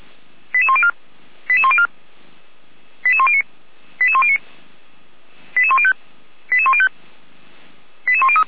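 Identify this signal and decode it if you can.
Five-tone selective call (ZVEI), heard as quick runs of five tones, tones of about 80 ms. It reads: 88104 88104 88109 88109 88105 88105 88103
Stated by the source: ZVEI five-tone call 88102 received by SDR, archived by Feuerwehr Orlamünde fire brigade.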